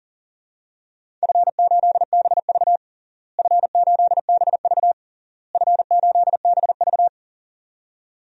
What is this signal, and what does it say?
Morse code, heard as a keyed tone, about 40 wpm, 700 Hz. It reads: F8BV F8BV F8BV
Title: DXCC-100 Call Signs - 4 Characters - Repeated 3x - 40wpm